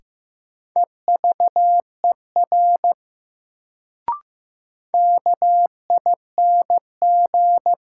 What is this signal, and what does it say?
Morse code, heard as a keyed tone, about 16 wpm, 700 Hz. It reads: EVER KING
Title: Sentences from Top 400 Words - New Word Drill - 15wpm